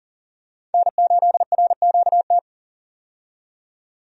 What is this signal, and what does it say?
Morse code, heard as a keyed tone, about 37 wpm, 700 Hz. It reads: N8RQT
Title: USA Call Signs - 5 Characters - 40wpm